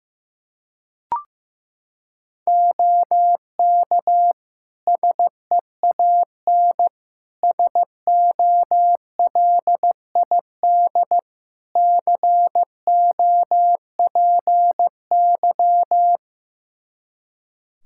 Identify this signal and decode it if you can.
Morse code, heard as a keyed tone, about 15 wpm, 700 Hz. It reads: OK SEAN SOLID COPY
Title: Practice QSO - Exchange 3 - More Info - 15wpm